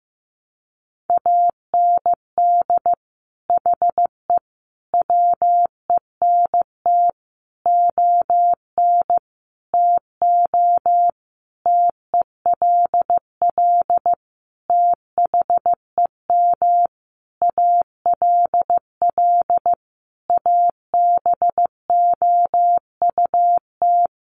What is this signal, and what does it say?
Morse code, heard as a keyed tone, about 14 wpm, 700 Hz. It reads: AND HE WENT ON TO TELL THEM ALL ABOUT
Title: Sentences from Top 900 Words 15wpm